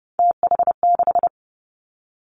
Morse code, heard as a keyed tone, about 30 wpm, 700 Hz. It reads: TH6
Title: Three Challenging Letter-Number Combinations 30wpm